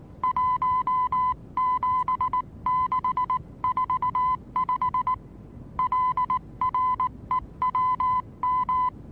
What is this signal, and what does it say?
Morse code, heard as a keyed tone, about 19 wpm, 1010 Hz. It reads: 17645 LREWM
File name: repos / Morsekurs / DJ4UF Morsen26